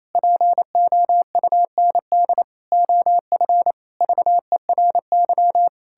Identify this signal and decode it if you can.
Morse code, heard as a keyed tone, about 28 wpm, 700 Hz. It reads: POUND OF VERY
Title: Pangram Challenge v2 - 28wpm